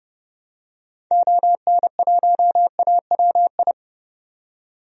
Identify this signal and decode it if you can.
Morse code, heard as a keyed tone, about 30 wpm, 700 Hz. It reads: ON1AWI
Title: DXCC-100 Call Signs - 6 Characters - 30wpm